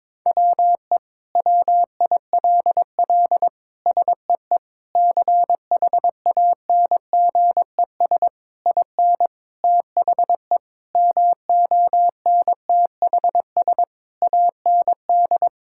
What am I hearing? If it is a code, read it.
Morse code, heard as a keyed tone, about 22 wpm, 700 Hz. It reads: WE WILL SEE CHANGES IN THE MONTHS AND
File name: Sentences with 7-Word Phrases - 22wpm